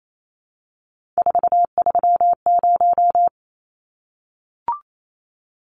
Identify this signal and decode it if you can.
Morse code, heard as a keyed tone, about 28 wpm, 700 Hz. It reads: 430
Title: Numbers - 3-Digit 28wpm